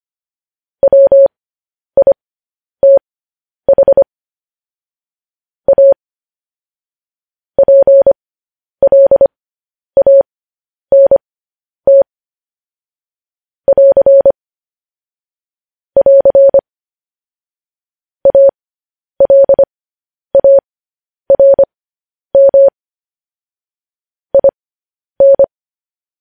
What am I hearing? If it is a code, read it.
Morse code, heard as a keyed tone, about 25 wpm, spaced as for 10 wpm, 550 Hz. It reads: WITH A PLANT <AR> <AR> ALARM IN